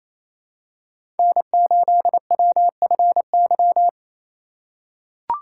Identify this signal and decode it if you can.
Morse code, heard as a keyed tone, about 28 wpm, 700 Hz. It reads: N8WFY
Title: USA Call Signs - 5 Characters - 2x Character Spacing - 28wpm